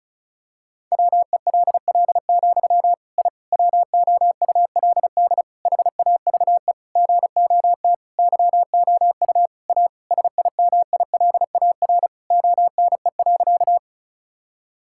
Morse code, heard as a keyed tone, about 34 wpm, 700 Hz. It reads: WELL, I WOULD HAVE GOT YOU A SIMILAR ONE.
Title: Sentences from Top 1000 Words - Review Part 5 - 35wpm